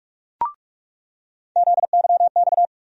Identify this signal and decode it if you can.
Morse code, heard as a keyed tone, about 47 wpm, 700 Hz. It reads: ZYX
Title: USA Call Sign Postfixes - 3 Letters 45wpm